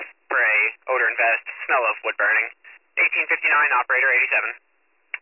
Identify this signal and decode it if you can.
Emergency Alert System (SAME) tones, heard as harsh buzzing bursts, with speech. It is heard from 0.3 to 4.2 s.